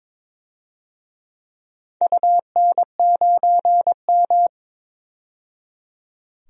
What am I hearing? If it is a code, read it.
Morse code, heard as a keyed tone, about 22 wpm, 700 Hz. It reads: UN9M
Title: DXCC-100 Call Signs - 4 Characters - 22wpm